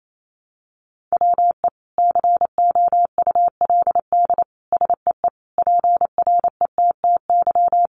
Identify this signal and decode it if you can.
Morse code, heard as a keyed tone, about 28 wpm, 700 Hz. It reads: WE COULD SEE PRETTY
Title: Sentences from Top 1000 Words 28wpm